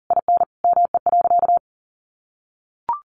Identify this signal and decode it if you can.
Morse code, heard as a keyed tone, about 37 wpm, 700 Hz. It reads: IN ME.
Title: Sentences from Top 1500 Words 40wpm